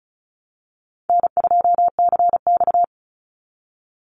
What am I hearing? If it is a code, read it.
Morse code, heard as a keyed tone, about 34 wpm, 700 Hz. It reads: N2CX
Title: Call Signs Repeated 3x - 35wpm